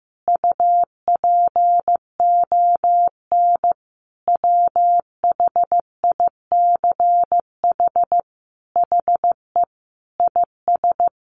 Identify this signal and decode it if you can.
Morse code, heard as a keyed tone, about 15 wpm, 700 Hz. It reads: UPON WHICH HE IS
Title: Sentences from Top 1200 Words 15wpm